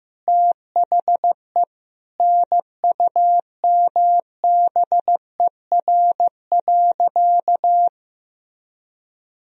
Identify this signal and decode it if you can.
Morse code, heard as a keyed tone, about 15 wpm, 700 Hz. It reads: THE NUMBER.
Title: Sentences from Top 1000 Words - Review Part 1 - 15wpm